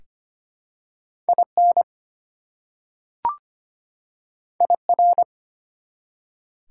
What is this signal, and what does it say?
Morse code, heard as a keyed tone, about 25 wpm, 700 Hz. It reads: IN IR
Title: Prefixes and Suffixes 25wpm